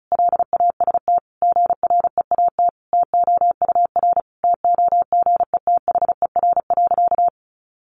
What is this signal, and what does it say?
Morse code, heard as a keyed tone, about 34 wpm, 700 Hz. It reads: LAST GREAT TOUR TOGETHER.
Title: Sentences from Top 1400 Words 35wpm